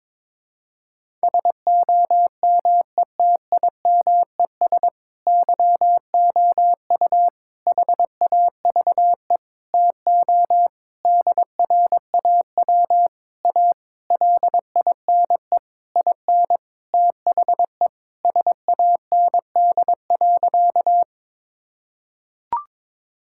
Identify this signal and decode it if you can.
Morse code, heard as a keyed tone, about 22 wpm, 700 Hz. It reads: SOMETIMES YOU HAVE TO DRAW A LINE IN THE SAND.
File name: Sentences with 7-Word Phrases - 22wpm